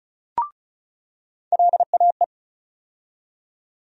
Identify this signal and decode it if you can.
Morse code, heard as a keyed tone, about 36 wpm, 700 Hz. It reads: LAE
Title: USA Call Sign Postfixes - 3 Letters 35wpm